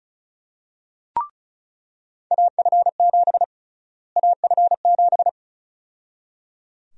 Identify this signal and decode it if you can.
Morse code, heard as a keyed tone, about 36 wpm, 700 Hz. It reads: AF7 AF7
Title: USA Call Sign Prefixes - Repeated 2x - 35wpm